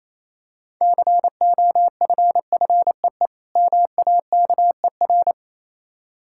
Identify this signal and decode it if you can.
Morse code, heard as a keyed tone, about 27 wpm, 700 Hz. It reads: COFFEE MAKER
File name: Easy Words - Speed-Racing with Hints - 22wpm